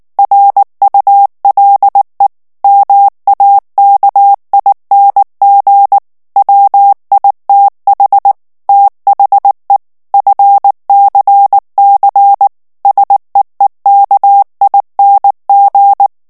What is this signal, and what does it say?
Morse code, heard as a keyed tone, about 19 wpm, 800 Hz. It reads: RULE MAKING WITH THE FCC SEEKING
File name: ANGLAIS texte clair 3 vitesse 18 mots minute